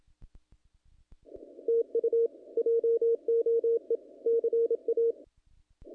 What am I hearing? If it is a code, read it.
Morse code, heard as a keyed tone, about 27 wpm, 460 Hz. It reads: TU JOE CA
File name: contests / naqp-cw-k5tr-op-k5pi- -1900